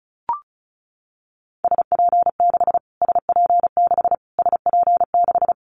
Morse code, heard as a keyed tone, about 34 wpm, 700 Hz. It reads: SP6 SP6 SP6
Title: Callsign Prefixes for Top 25 DXCC - Repeated 3x - 35wpm